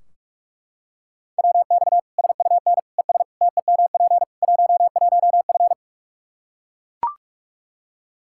Morse code, heard as a keyed tone, about 41 wpm, 700 Hz. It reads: WX SUN ES TEMP 11F
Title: Practice QSO - Exchange 2 - Info - 45wpm